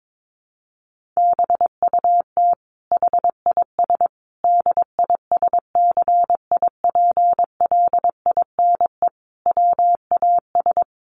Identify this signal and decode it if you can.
Morse code, heard as a keyed tone, about 22 wpm, 700 Hz. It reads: BUT HIS DISCIPLINE WAS
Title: Sentences from Top 1400 Words 22wpm